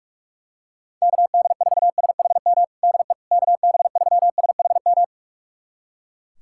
Morse code, heard as a keyed tone, about 42 wpm, 700 Hz. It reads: KD4SHM DE KB3SHM
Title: USA Callsign DE Callsign - 6 Characters - 45wpm